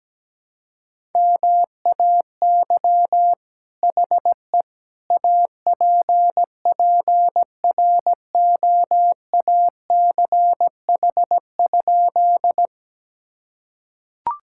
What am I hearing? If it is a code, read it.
Morse code, heard as a keyed tone, about 17 wpm, 700 Hz. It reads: MAY HE APPROACH?
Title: Sentences from Top 1000 Words - Review Part 6 - 17wpm